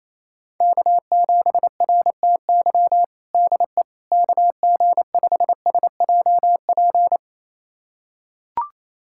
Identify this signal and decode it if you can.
Morse code, heard as a keyed tone, about 28 wpm, 700 Hz. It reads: K7RTY DE KG5SJP
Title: USA Callsign DE Callsign 28wpm